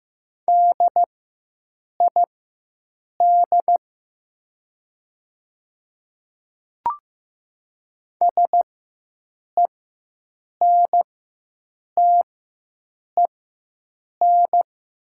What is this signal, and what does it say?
Morse code, heard as a keyed tone, about 15 wpm, spaced as for 7 wpm, 700 Hz. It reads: DID SENTEN
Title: 4x Character Spacing - Top 200 Words - 15wpm